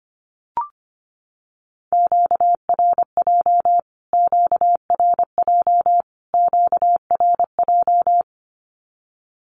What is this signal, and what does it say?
Morse code, heard as a keyed tone, about 25 wpm, 700 Hz. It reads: QRJ QRJ QRJ
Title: Q-Codes Repeated 3x 25wpm